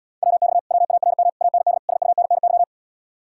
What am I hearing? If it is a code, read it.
Morse code, heard as a keyed tone, about 77 wpm, 700 Hz. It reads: WX WARM AND SUNNY